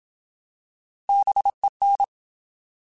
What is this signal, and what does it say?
Morse code, heard as a keyed tone, about 27 wpm, 800 Hz. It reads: BEN
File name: Alphabetical / Alphabetical mp3s / Bs @25